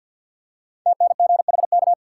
Morse code, heard as a keyed tone, about 52 wpm, 700 Hz. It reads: TNGHK
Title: Code Groups - 5 Letter:Numbers - 50wpm